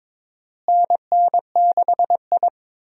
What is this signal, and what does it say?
Morse code, heard as a keyed tone, about 23 wpm, 700 Hz. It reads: NN6I
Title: USA Call Signs - 4 Characters - Repeated 3x - 22wpm